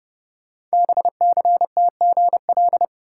Morse code, heard as a keyed tone, about 29 wpm, 700 Hz. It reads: BCTGL
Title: Code Groups - 5 Letters - 30wpm